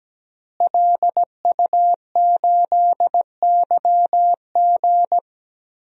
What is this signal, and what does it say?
Morse code, heard as a keyed tone, about 17 wpm, 700 Hz. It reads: LU8YG
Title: DXCC-100 Call Signs - 5 Characters - Repeated 3x - 17wpm